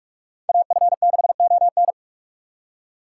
Morse code, heard as a keyed tone, about 46 wpm, 700 Hz. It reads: AF6ON
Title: USA Call Signs - 5 Characters - 3x Character Spacing - 45wpm